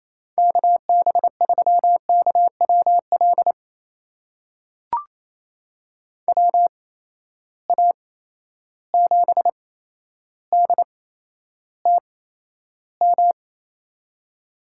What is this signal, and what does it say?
Morse code, heard as a keyed tone, about 28 wpm, spaced as for 12 wpm, 700 Hz. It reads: KB3KWL WA7DTM
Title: USA Call Signs - 6 Characters - 8x Character Spacing - 28wpm